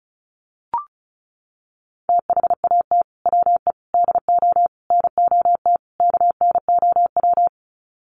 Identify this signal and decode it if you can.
Morse code, heard as a keyed tone, about 34 wpm, 700 Hz. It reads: THAT WE DO NOT KNOW
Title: Sets of 5 Words Encore2 35wpm